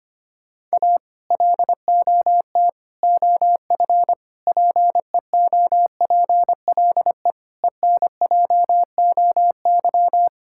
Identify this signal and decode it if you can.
Morse code, heard as a keyed tone, about 25 wpm, 700 Hz. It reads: A LOT OF PEOPLE ENJOY
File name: practice / 25wpm / Sentences with 4-Word Phrases - 25wpm